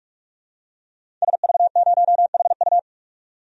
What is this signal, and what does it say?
Morse code, heard as a keyed tone, about 44 wpm, 700 Hz. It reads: SV0HU
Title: DXCC-100 Call Signs - 5 Characters - 45wpm